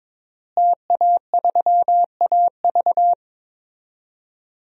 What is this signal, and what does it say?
Morse code, heard as a keyed tone, about 22 wpm, 700 Hz. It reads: TA3AV
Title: DXCC-100 Call Signs - 5 Characters - Repeated 2x - 22wpm